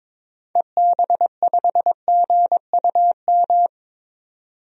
Morse code, heard as a keyed tone, about 22 wpm, 700 Hz. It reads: EB5GUM
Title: DXCC-100 Call Signs - 6 Characters - 22wpm